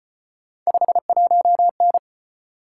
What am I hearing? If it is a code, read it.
Morse code, heard as a keyed tone, about 32 wpm, 700 Hz. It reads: 51N